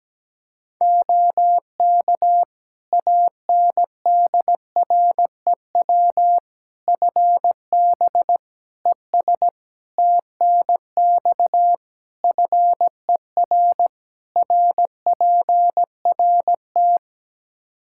Morse code, heard as a keyed tone, about 17 wpm, 700 Hz. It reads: OK ANDREW FB ES TNX FER RPRT
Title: Practice QSO - Exchange 2 - Info - 17wpm